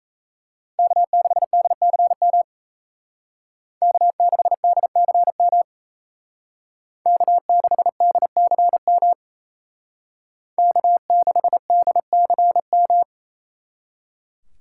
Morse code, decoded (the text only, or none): K6DCM K6DCM K6DCM K6DCM